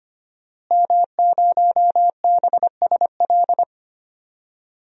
Morse code, heard as a keyed tone, about 25 wpm, 700 Hz. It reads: M0BSL
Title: DXCC-100 Call Signs - 5 Characters - Repeated 2x - 25wpm